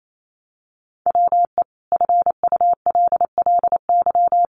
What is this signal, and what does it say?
Morse code, heard as a keyed tone, about 28 wpm, 700 Hz. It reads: WE FULLY
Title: Sentences from Top 1300 Words 28wpm